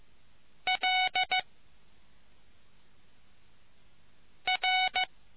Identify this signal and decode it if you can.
Morse code, heard as a keyed tone, about 15 wpm, 740 Hz. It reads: L R